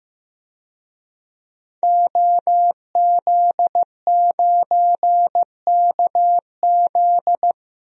Morse code, heard as a keyed tone, about 15 wpm, 700 Hz. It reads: OZ9KZ
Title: DXCC-100 Call Signs - 5 Characters - 15wpm